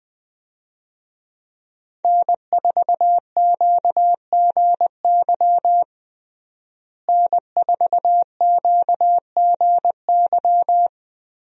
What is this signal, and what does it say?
Morse code, heard as a keyed tone, about 20 wpm, 700 Hz. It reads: N4QGY N4QGY
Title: USA Call Signs - 5 Characters - Repeated 2x - 20wpm